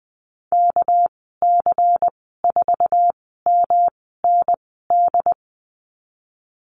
Morse code, heard as a keyed tone, about 20 wpm, spaced as for 15 wpm, 700 Hz. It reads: KC4MND
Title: 207 Lesson 33g - Letter j - Call Signs 20wpm